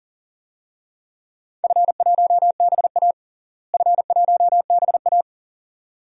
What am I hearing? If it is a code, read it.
Morse code, heard as a keyed tone, about 40 wpm, 700 Hz. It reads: F1BA F1BA
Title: DXCC-100 Call Signs - 4 Characters - Repeated 2x - 40wpm